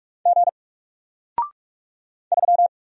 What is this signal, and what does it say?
Morse code, heard as a keyed tone, about 46 wpm, 700 Hz. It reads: G 3
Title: Single Letter-Number 45wpm